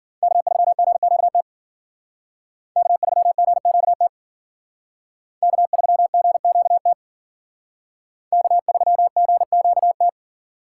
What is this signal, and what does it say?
Morse code, heard as a keyed tone, about 51 wpm, 700 Hz. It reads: K3GQT K3GQT K3GQT K3GQT